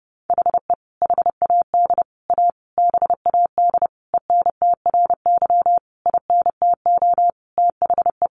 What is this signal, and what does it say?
Morse code, heard as a keyed tone, about 30 wpm, 700 Hz. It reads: HE HAD A BAD ENTRY INTO THE
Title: Sentences from Top 1500 Words 30wpm